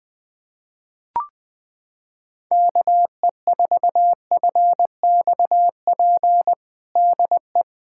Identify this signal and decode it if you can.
Morse code, heard as a keyed tone, about 20 wpm, 700 Hz. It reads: KE4FXP DE
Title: USA Callsign DE Callsign - 6 Characters - 20wpm